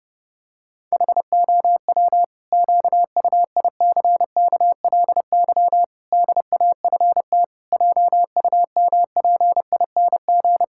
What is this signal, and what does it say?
Morse code, heard as a keyed tone, about 30 wpm, 700 Hz. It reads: HOW QUICKLY DAFT JUMPING
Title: Pangram Challenge v2 - 30wpm